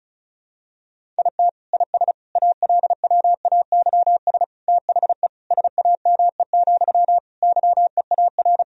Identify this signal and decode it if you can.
Morse code, heard as a keyed tone, about 34 wpm, 690 Hz. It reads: IT IS ALWAYS THE SAME, YEAR
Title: Sentences from Top 1000 Words - Review Part 1 - 35wpm